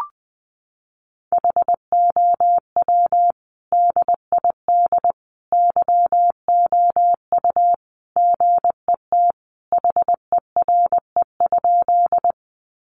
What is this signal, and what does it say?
Morse code, heard as a keyed tone, about 20 wpm, 700 Hz. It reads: HOW DID YOU GET HERE?
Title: Sentences from Top 200 Words 20wpm